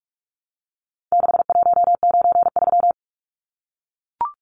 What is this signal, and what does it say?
Morse code, heard as a keyed tone, about 44 wpm, 700 Hz. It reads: 6193
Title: Numbers - 4-Digit 45wpm